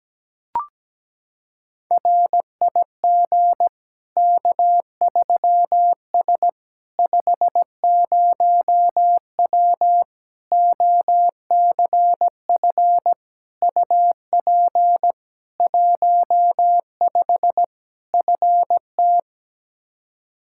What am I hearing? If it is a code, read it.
Morse code, heard as a keyed tone, about 17 wpm, 700 Hz. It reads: RIG K3S 50W OCF UP 15 FT
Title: Practice QSO - Exchange 2 - Info - 17wpm